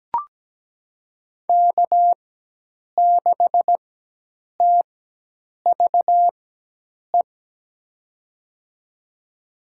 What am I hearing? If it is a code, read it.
Morse code, heard as a keyed tone, about 17 wpm, spaced as for 8 wpm, 700 Hz. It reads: K6TVE
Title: USA Call Signs - 4x Character Spacing - 17wpm